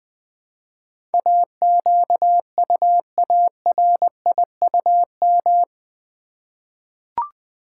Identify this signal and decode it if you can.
Morse code, heard as a keyed tone, about 20 wpm, 700 Hz. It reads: AQUARIUM
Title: Easy Words with Hints - 20wpm